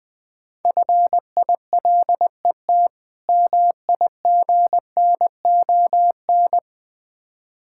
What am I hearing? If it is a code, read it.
Morse code, heard as a keyed tone, about 21 wpm, 690 Hz. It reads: FILET MIGNON
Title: Difficult Words with Hints - 20wpm